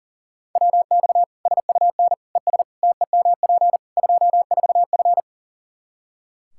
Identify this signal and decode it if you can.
Morse code, heard as a keyed tone, about 40 wpm, 700 Hz. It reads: WX SUN ES TEMP 24F
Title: Practice QSO - Exchange 2 - Info - 40wpm